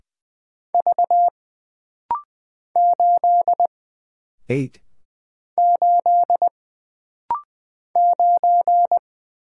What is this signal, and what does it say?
Morse code, heard as a keyed tone, about 20 wpm, 700 Hz. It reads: V 8 8 9
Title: Single Letter-Number - Rapid-Fire 20wpm